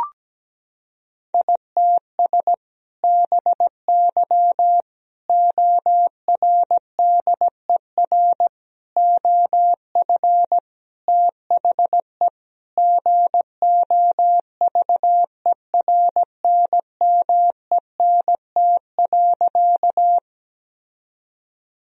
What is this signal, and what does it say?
Morse code, heard as a keyed tone, about 17 wpm, 700 Hz. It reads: ITS BY ORDER OF THE GOVERNMENT.
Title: Sentences from Top 1000 Words - Review Part 3 - 17wpm